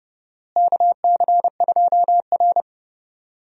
Morse code, heard as a keyed tone, about 29 wpm, 700 Hz. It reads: KC2R
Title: USA Call Signs - 4 Characters - Repeated 2x - 30wpm